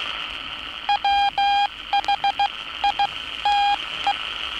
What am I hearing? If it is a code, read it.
Morse code, heard as a keyed tone, about 17 wpm, spaced as for 14 wpm, 810 Hz, with noise. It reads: WHITE
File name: training / NAVPERS 11049to11060 / NAVPERS 11059 RA RB